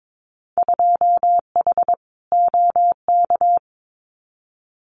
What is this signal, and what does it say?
Morse code, heard as a keyed tone, about 22 wpm, 700 Hz. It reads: 2H OK
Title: Winter Field Day Exchanges Encore 22wpm